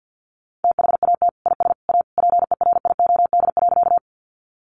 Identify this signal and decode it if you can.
Morse code, heard as a keyed tone, about 50 wpm, 700 Hz. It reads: THAT IS A PERIOD.
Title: Sentences from Top 800 Words 50wpm